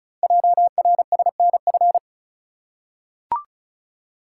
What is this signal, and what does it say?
Morse code, heard as a keyed tone, about 36 wpm, 700 Hz. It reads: JRSNF
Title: Code Groups - 5 Letters - 35wpm